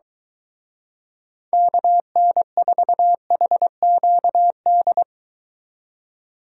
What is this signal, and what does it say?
Morse code, heard as a keyed tone, about 23 wpm, 700 Hz. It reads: KN4HQD